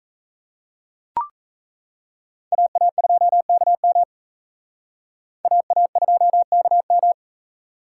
Morse code, heard as a keyed tone, about 35 wpm, 700 Hz. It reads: AA2KM AA2KM
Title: Speed Racing - USA Call Signs 28wpm